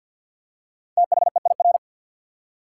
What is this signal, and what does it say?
Morse code, heard as a keyed tone, about 48 wpm, 700 Hz. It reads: THEIR
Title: Top 180 QSO Elements 50wpm